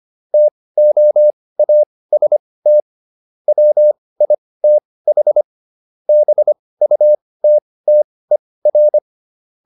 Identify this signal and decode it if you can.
Morse code, heard as a keyed tone, about 25 wpm, spaced as for 18 wpm, 600 Hz. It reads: TOAST WITH BUTTER